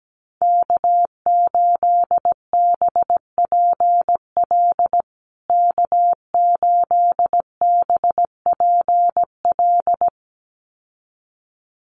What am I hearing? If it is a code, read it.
Morse code, heard as a keyed tone, about 17 wpm, 700 Hz. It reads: K8BPL K8BPL
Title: USA Call Signs Repeated 2x 17wpm